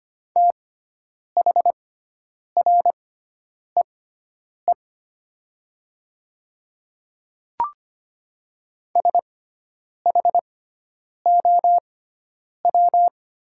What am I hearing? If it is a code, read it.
Morse code, heard as a keyed tone, about 25 wpm, spaced as for 9 wpm, 700 Hz. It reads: THREE SHOW
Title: 6x Character Spacing - Top 500 Words - 25wpm